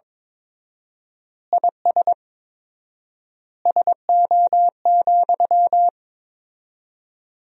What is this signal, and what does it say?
Morse code, heard as a keyed tone, about 22 wpm, 700 Hz. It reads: IS SO,
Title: 4x Word-Spacing - Sentences from Top 200 Words - 22wpm